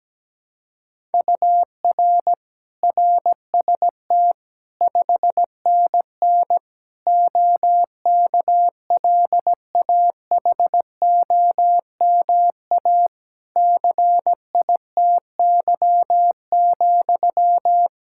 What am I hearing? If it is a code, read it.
Morse code, heard as a keyed tone, about 17 wpm, 700 Hz. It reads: UR RST 5NN OKLAHOMA CITY,
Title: Practice QSO - Exchange 1 - Beginning - 17wpm